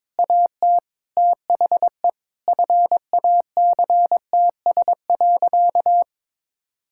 Morse code, heard as a keyed tone, about 22 wpm, 700 Hz. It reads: AT THE FACTS.
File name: practice / 22wpm / Sentences with 8-Word Phrases - 22wpm